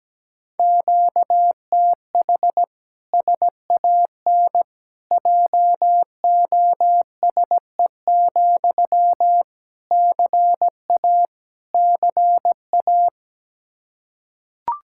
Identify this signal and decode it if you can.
Morse code, heard as a keyed tone, about 17 wpm, 700 Hz. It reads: QTH SAN JOSE, CA CA
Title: Practice QSO - Exchange 1 - Beginning - 17wpm